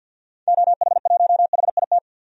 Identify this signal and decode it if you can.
Morse code, heard as a keyed tone, about 52 wpm, 700 Hz. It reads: OH1HIT
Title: DXCC-100 Call Signs - 6 Characters - 50wpm